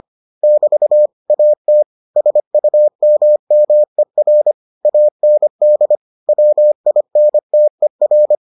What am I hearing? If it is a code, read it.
Morse code, heard as a keyed tone, about 25 wpm, 600 Hz. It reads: <BT> AT SUMMER AND WINTER